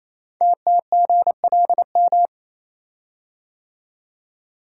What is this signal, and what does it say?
Morse code, heard as a keyed tone, about 28 wpm, 700 Hz. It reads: TTGLM
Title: Code Groups - 5 Letters - 28wpm